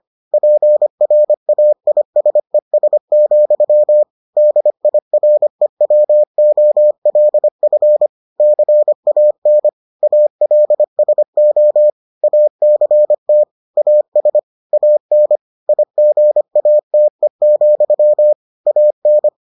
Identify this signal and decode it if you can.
Morse code, heard as a keyed tone, about 25 wpm, 600 Hz. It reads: PRAISES, DIREWOLF CAN ALSO ACT AS AN IGATE, AN